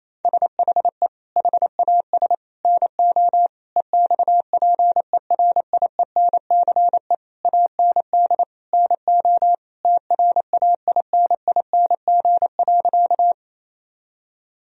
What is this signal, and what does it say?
Morse code, heard as a keyed tone, about 28 wpm, 700 Hz. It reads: SHE HAS NO EXPERIENCE AND NO TRAINING.
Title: Sentences from Top 1000 Words - Review Part 4 - 28wpm